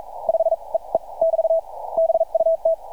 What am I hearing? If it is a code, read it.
Morse code, heard as a keyed tone, about 45 wpm, spaced as for 36 wpm, 650 Hz. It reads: 5 EE <BT> BUT
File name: archived / CRSnet